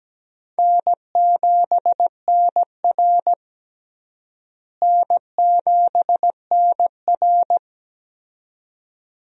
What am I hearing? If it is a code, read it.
Morse code, heard as a keyed tone, about 17 wpm, 700 Hz. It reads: N7NR N7NR
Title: USA Call Signs - 4 Characters - Repeated 2x - 17wpm